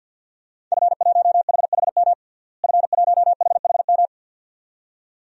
Morse code, heard as a keyed tone, about 52 wpm, 700 Hz. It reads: F1HHM F1HHM
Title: DXCC-100 Call Signs - 5 Characters - Repeated 2x - 50wpm